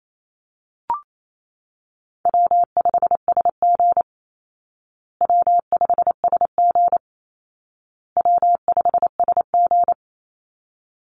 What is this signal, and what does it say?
Morse code, heard as a keyed tone, about 28 wpm, 700 Hz. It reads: W5SG W5SG W5SG
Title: Call Signs Repeated 2x - 28wpm